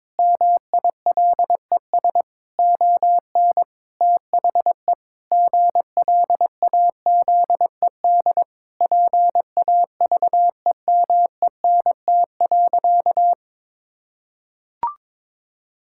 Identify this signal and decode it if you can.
Morse code, heard as a keyed tone, about 22 wpm, 700 Hz. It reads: MILES ON THE GLAZED PAVEMENT.
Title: Pangram Challenge v2 - 22wpm